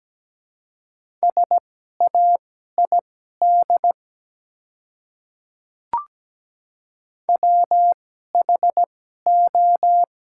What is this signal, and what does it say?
Morse code, heard as a keyed tone, about 17 wpm, spaced as for 12 wpm, 690 Hz. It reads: SAID WHO
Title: 2x Character Spacing - Top 100 Words - 17wpm